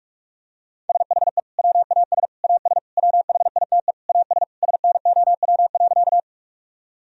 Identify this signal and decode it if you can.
Morse code, heard as a keyed tone, about 46 wpm, 700 Hz. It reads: SHE WAS AS WHITE AS SNOW.
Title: Sentences from Top 1000 Words - Review Part 3 - 45wpm